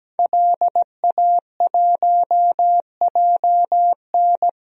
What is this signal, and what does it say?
Morse code, heard as a keyed tone, about 17 wpm, 700 Hz. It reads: LA1JN
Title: DXCC-100 Call Signs - 5 Characters - 17wpm